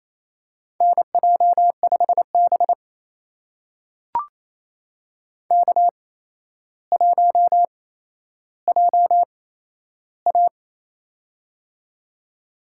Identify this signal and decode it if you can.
Morse code, heard as a keyed tone, about 28 wpm, spaced as for 12 wpm, 700 Hz. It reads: NJ5B K1JA
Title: USA Call Signs - 4 Characters - 8x Character Spacing - 28wpm